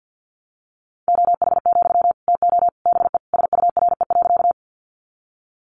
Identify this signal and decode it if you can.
Morse code, heard as a keyed tone, about 50 wpm, 700 Hz. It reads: OH, TO BE SURE.
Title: Sentences from Top 500 Words 50wpm